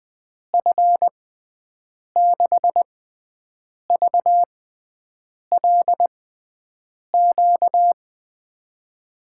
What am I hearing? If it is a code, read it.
Morse code, heard as a keyed tone, about 20 wpm, spaced as for 7 wpm, 700 Hz. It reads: F6VLQ
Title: USA Call Signs - 6x Character Spacing - 20wpm